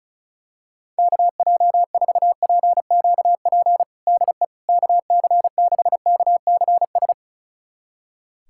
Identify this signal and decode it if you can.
Morse code, heard as a keyed tone, about 34 wpm, 700 Hz. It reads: KJ4PQP DE KC6KCS
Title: USA Callsign DE Callsign - 6 Characters - 35wpm